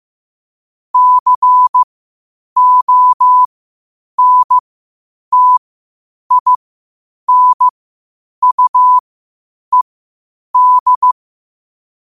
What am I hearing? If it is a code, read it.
Morse code, heard as a keyed tone, about 15 wpm, spaced as for 8 wpm, 1000 Hz. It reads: CONTINUED
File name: qotdfiles / 8wpm